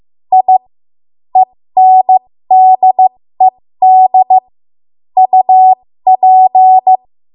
Morse code, heard as a keyed tone, about 15 wpm, spaced as for 13 wpm, 750 Hz. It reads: I ENDED UP
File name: Morse / 13WPM